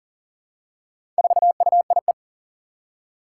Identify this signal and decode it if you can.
Morse code, heard as a keyed tone, about 40 wpm, 700 Hz. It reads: 4UIE
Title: Four Challenging Letter-Number Combinations 40wpm